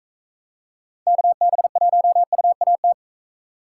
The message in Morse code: KB1UAT

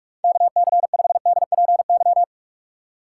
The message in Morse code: KC5DPY

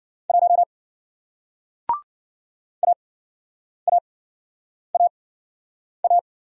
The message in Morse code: 1 AAAA

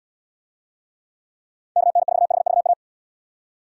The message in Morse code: KN4HVA

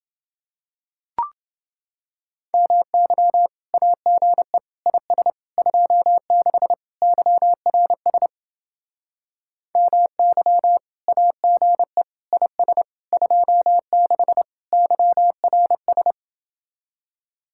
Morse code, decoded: MY AGE IS 26 YRS MY AGE IS 26 YRS

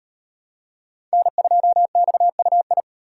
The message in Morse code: N2XUI

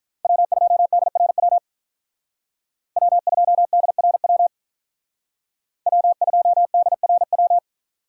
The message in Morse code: W2DRW W2DRW W2DRW

